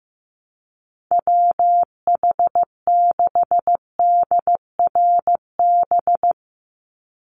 Morse code, heard as a keyed tone, about 15 wpm, 700 Hz. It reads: WH6DRB
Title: USA Call Signs 15wpm